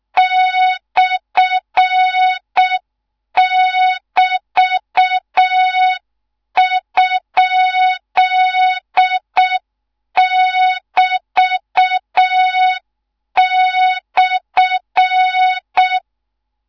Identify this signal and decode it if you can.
Morse code, heard as a keyed tone, about 6 wpm, 3700 Hz, with noise. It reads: /<BT>?<BT>/